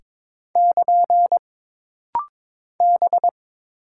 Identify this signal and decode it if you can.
Morse code, heard as a keyed tone, about 22 wpm, 700 Hz. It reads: <KN> B
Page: This rapid-fire timing is geared to push your ICR (Instant Character Recognition) skills.